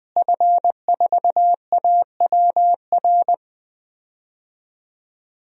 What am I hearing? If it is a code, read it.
Morse code, heard as a keyed tone, about 20 wpm, 700 Hz. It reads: F4AWR
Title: DXCC-100 Call Signs - 5 Characters - Repeated 2x - 20wpm